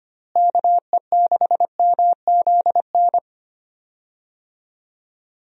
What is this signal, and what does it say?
Morse code, heard as a keyed tone, about 25 wpm, 700 Hz. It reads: KE6MZN
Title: USA Call Signs - 6 Characters - Repeated 3x - 25wpm